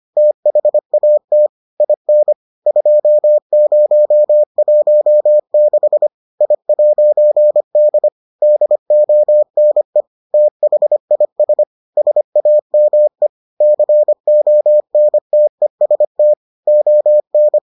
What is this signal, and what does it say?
Morse code, heard as a keyed tone, about 25 wpm, 600 Hz. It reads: THAT IN 2016 I'D DONE THIS SAME CONTEST ON